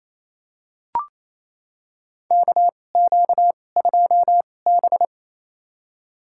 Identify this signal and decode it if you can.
Morse code, heard as a keyed tone, about 28 wpm, spaced as for 20 wpm, 700 Hz. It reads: KQ2B
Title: USA Call Signs - 4 Characters - 2x Character Spacing - 28wpm